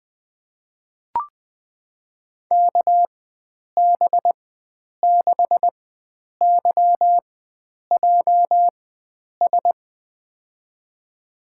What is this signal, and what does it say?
Morse code, heard as a keyed tone, about 21 wpm, spaced as for 9 wpm, 700 Hz. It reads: KB6YJS
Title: USA Call Signs - 4x Character Spacing - 20wpm